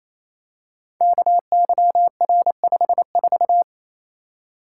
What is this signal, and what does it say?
Morse code, heard as a keyed tone, about 27 wpm, 700 Hz. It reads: KYR54
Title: Code Groups - 5 Letter:Numbers - 28wpm